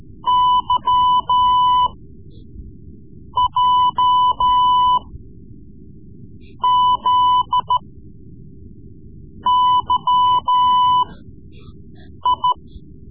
Morse code, heard as a keyed tone, about 14 wpm, spaced as for 5 wpm, 1000 Hz. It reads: YJZYI
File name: Morse Lessons / Morse Lessons / Tape-3B